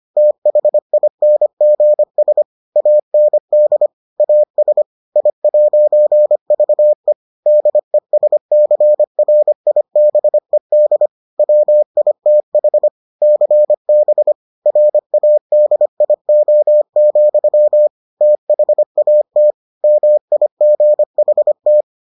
Morse code, heard as a keyed tone, about 25 wpm, 600 Hz. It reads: THINGS AND AS I'VE DESCRIBED WITH CB RADIO, THAT MIGHT